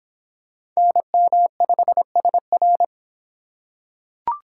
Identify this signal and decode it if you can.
Morse code, heard as a keyed tone, about 25 wpm, 700 Hz. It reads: NM5SR